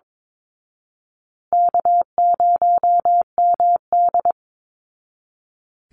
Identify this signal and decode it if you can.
Morse code, heard as a keyed tone, about 22 wpm, 700 Hz. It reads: K0MD
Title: Easy Call Signs Repeated 2x - 22wpm